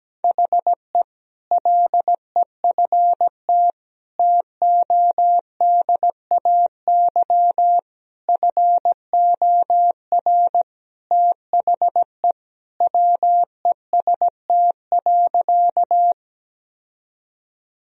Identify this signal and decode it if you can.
Morse code, heard as a keyed tone, about 17 wpm, 700 Hz. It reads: HE LEFT TODAY FOR THE WEST.
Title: Sentences from Top 1000 Words - Review Part 2 - 17wpm